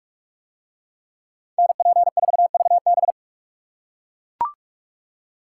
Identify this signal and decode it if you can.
Morse code, heard as a keyed tone, about 43 wpm, 700 Hz. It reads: NP4VB